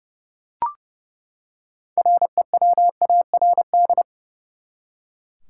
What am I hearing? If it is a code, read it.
Morse code, heard as a keyed tone, about 30 wpm, 700 Hz. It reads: REWARD
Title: Six Letter Words - 30wpm